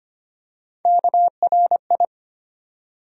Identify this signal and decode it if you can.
Morse code, heard as a keyed tone, about 25 wpm, 700 Hz. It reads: KRI
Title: USA Call Sign Postfixes - 3 Letters 25wpm